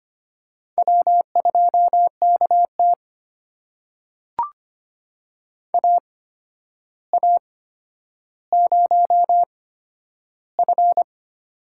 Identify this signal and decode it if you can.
Morse code, heard as a keyed tone, about 25 wpm, spaced as for 11 wpm, 700 Hz. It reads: W2KT AA0F
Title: USA Call Signs - 4 Characters - 8x Character Spacing - 25wpm